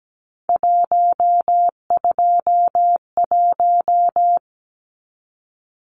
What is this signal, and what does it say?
Morse code, heard as a keyed tone, about 17 wpm, 700 Hz. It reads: 121
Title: Numbers - 3-Digit 17wpm